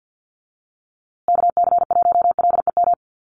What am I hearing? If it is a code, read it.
Morse code, heard as a keyed tone, about 50 wpm, 700 Hz. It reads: KC1LEM
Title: USA Call Signs Repeated 3x 50wpm